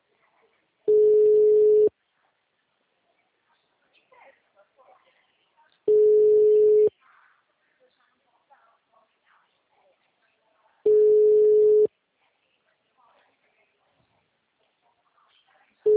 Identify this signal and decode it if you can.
Ringback tone (Europe, 425 Hz), with speech.